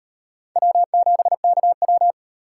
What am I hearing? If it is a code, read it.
Morse code, heard as a keyed tone, about 36 wpm, 700 Hz. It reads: W7KW